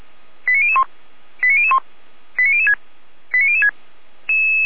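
Five-tone selective call (ZVEI), heard as quick runs of five tones, tones of about 70 ms. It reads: 89001 89001 89006 89006
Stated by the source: ZVEI five-tone call 89017 received on SDR.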